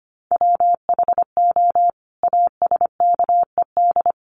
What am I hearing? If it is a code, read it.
Morse code, heard as a keyed tone, about 25 wpm, 700 Hz. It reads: WHO ASKED
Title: Sets of 5 Words Encore2 25wpm